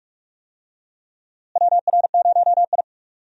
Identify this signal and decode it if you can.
Morse code, heard as a keyed tone, about 41 wpm, 700 Hz. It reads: WR0I
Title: USA Call Signs - 4 Characters - Repeated 3x - 45wpm